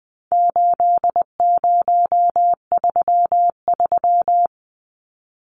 Morse code, heard as a keyed tone, about 20 wpm, 700 Hz. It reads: 8033
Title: Numbers - 4-Digit 20wpm